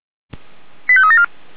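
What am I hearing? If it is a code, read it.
Five-tone selective call (ZVEI), heard as quick runs of five tones, tones of about 80 ms. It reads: 85374
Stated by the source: Received on SDR, ZVEI five-tone selective call 89200 for the Orlamünde fire brigade (feuerwehr-orlamuende).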